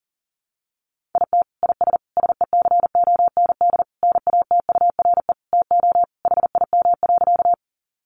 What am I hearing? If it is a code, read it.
Morse code, heard as a keyed tone, about 40 wpm, 700 Hz. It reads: IT IS SECOND NATURE TO HIM.
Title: Sentences from Top 600 Words 40wpm